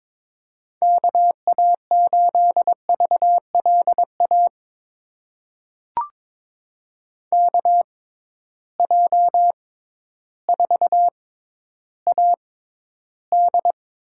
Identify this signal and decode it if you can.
Morse code, heard as a keyed tone, about 22 wpm, spaced as for 12 wpm, 700 Hz. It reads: KA8VLA KJ4AD
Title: USA Call Signs - 6 Characters - 6x Character Spacing - 22wpm